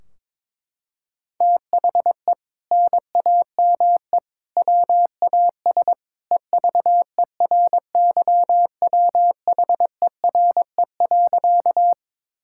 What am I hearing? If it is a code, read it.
Morse code, heard as a keyed tone, about 22 wpm, 700 Hz. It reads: THE NAME WAS EVERYWHERE.
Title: Sentences from Top 1600 Words 22wpm